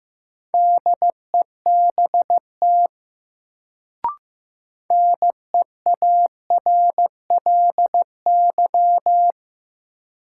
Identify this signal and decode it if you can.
Morse code, heard as a keyed tone, about 15 wpm, 700 Hz. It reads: DEBT NEARLY
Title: Sentences from Top 1300 Words - New Word Drill - 15wpm